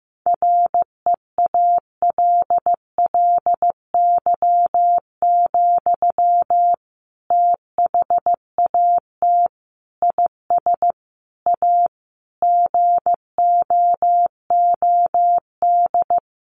Morse code, heard as a keyed tone, about 15 wpm, 700 Hz. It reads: REALLY, THAT IS A GOOD
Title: Sentences from Top 600 Words 15wpm